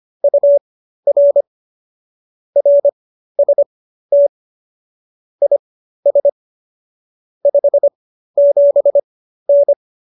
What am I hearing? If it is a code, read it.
Morse code, heard as a keyed tone, about 25 wpm, spaced as for 13 wpm, 570 Hz. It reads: UR RST IS 57N
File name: qso215b 13